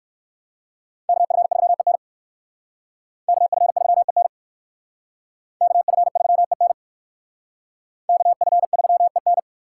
Morse code, heard as a keyed tone, about 56 wpm, 700 Hz. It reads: KF3EN KF3EN KF3EN KF3EN